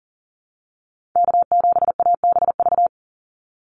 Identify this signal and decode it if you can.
Morse code, heard as a keyed tone, about 40 wpm, 700 Hz. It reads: K7ABV
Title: Call Signs Repeated 3x - 40wpm